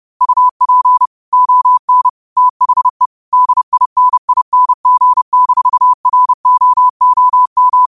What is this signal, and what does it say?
Morse code, heard as a keyed tone, about 30 wpm, 1000 Hz. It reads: UP ON THE DINING-ROOM
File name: qotdfiles / 30wpm